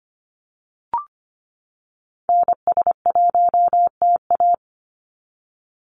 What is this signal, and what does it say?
Morse code, heard as a keyed tone, about 25 wpm, 700 Hz. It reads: NS1TA
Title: NA SOTA Activator Callsigns 25wpm